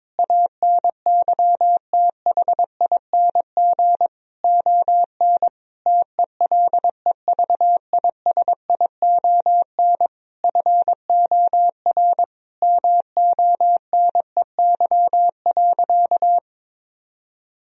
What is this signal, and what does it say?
Morse code, heard as a keyed tone, about 22 wpm, 700 Hz. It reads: ANYTHING ON TELEVISION FOR MONEY.
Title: Sentences from Top 1000 Words - Review Part 4 - 22wpm